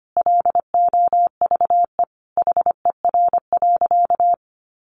Morse code, heard as a keyed tone, about 25 wpm, 700 Hz. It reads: LOVE HER.
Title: Sentences from Top 400 Words 25wpm